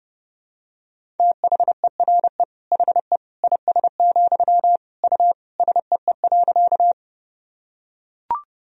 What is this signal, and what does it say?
Morse code, heard as a keyed tone, about 30 wpm, 700 Hz. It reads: THERE HE IS, U SEE.
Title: Sentences from Top 100 Words and Abbreviations 30wpm